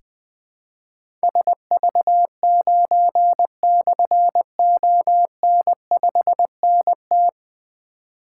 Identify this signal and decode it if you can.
Morse code, heard as a keyed tone, about 20 wpm, 700 Hz. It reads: SV9/ON5NT
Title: Difficult Call Signs 20wpm